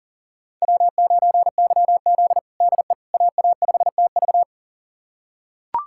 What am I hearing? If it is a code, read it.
Morse code, heard as a keyed tone, about 40 wpm, 700 Hz. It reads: W9YZ DE AA5TV
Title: USA Callsign DE Callsign 40wpm